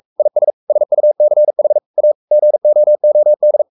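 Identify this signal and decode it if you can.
Morse code, heard as a keyed tone, about 44 wpm, 600 Hz. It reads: IS SUCH A GOOD
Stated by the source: Morse code at 43 wpm sending Ken Follett's novel Never in English, from CW med Gnister.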